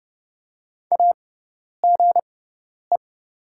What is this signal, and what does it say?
Morse code, heard as a keyed tone, about 30 wpm, spaced as for 10 wpm, 690 Hz. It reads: AGE
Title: 6x Character Spacing - Top 500 Words - 30wpm